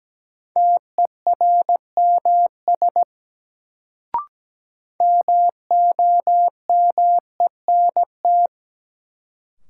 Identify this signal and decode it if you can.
Morse code, heard as a keyed tone, about 17 wpm, 700 Hz. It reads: TERMS MOMENT
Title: Sets of 3 Words - New Word Drill - 17wpm